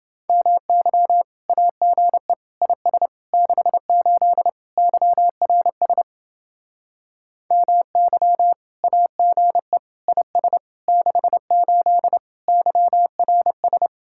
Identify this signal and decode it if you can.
Morse code, heard as a keyed tone, about 29 wpm, 700 Hz. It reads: MY AGE IS 68 YRS MY AGE IS 68 YRS